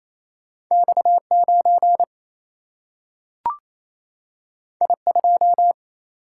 Morse code, heard as a keyed tone, about 28 wpm, 700 Hz. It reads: X9 I2
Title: Single-Letter Single-Number 28wpm